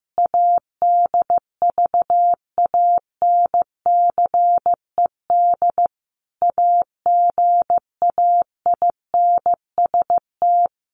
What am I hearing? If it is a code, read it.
Morse code, heard as a keyed tone, about 15 wpm, 700 Hz. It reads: ADVANCED AGAINST
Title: Sentences from Top 1200 Words 15wpm